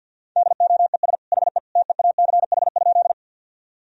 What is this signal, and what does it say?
Morse code, heard as a keyed tone, about 50 wpm, 700 Hz. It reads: DOES HE TEACH?
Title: Sentences from Top 1000 Words - Review Part 2 - 50wpm